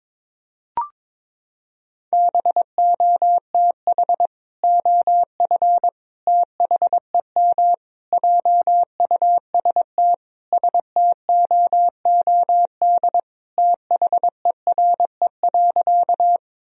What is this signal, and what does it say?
Morse code, heard as a keyed tone, about 22 wpm, 700 Hz. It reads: BOTH OF THEM JUST STOOD THERE.
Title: Sentences from Top 1000 Words - Review Part 2 - 22wpm